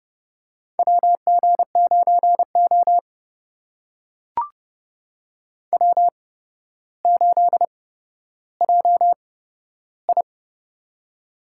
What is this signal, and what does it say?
Morse code, heard as a keyed tone, about 30 wpm, spaced as for 13 wpm, 700 Hz. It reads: WG9O W8JI
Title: USA Call Signs - 4 Characters - 8x Character Spacing - 30wpm